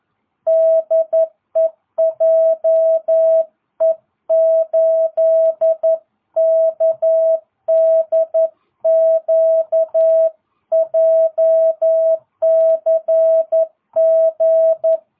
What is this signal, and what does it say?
Morse code, heard as a keyed tone, about 11 wpm, 650 Hz. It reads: DE JE8KDQ JCG